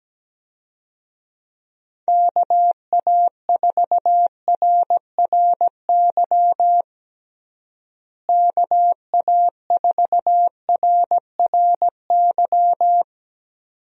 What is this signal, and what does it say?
Morse code, heard as a keyed tone, about 17 wpm, 700 Hz. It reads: KA4RRY KA4RRY
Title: USA Call Signs - 6 Characters - Repeated 2x - 17wpm